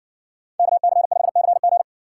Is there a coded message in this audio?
Yes: Morse code from 0.6 to 1.8 s, heard as a keyed tone, about 53 wpm, 700 Hz. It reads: KC5CG